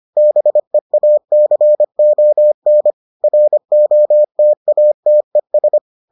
Morse code, heard as a keyed tone, about 25 wpm, 600 Hz. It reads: BEACON ROTATES